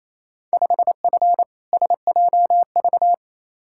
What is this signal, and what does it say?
Morse code, heard as a keyed tone, about 28 wpm, 700 Hz. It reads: 5F SJV
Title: Field Day Exchanges Encore - 28wpm